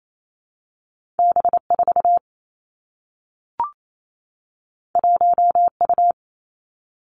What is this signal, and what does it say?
Morse code, heard as a keyed tone, about 28 wpm, 700 Hz. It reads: B 4 1 U
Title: Two Challenging Letter-Number Combinations 28wpm